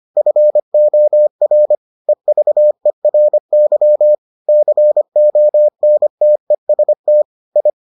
Morse code, heard as a keyed tone, about 25 wpm, 600 Hz. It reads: FOR EVERY CONTEST I